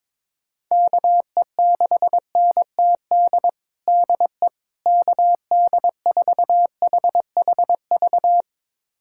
Morse code, heard as a keyed tone, about 22 wpm, 700 Hz. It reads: KE6NTD DE KD4HHV
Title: USA Callsign DE Callsign 22wpm